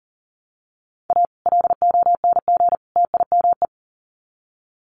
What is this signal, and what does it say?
Morse code, heard as a keyed tone, about 40 wpm, 700 Hz. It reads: A LONG TIME
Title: Sets of 3 Words 40wpm